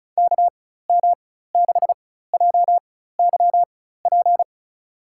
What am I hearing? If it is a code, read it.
Morse code, heard as a keyed tone, about 34 wpm, spaced as for 16 wpm, 700 Hz. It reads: KM6JYP
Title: USA Call Signs - 4x Character Spacing - 35wpm